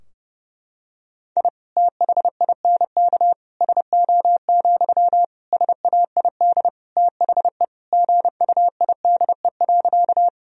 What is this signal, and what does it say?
Morse code, heard as a keyed tone, about 30 wpm, 700 Hz. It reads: I THINK SO, SAID THE GUIDE.
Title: Sentences from Top 1000 Words - Review Part 3 - 30wpm